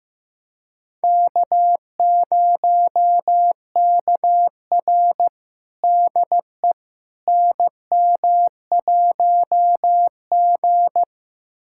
Morse code, heard as a keyed tone, about 15 wpm, 700 Hz. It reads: K0KR DE NM1G
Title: USA Callsign DE Callsign - 4 Characters - 15wpm